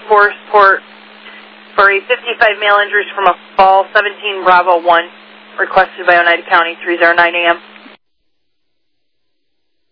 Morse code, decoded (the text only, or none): none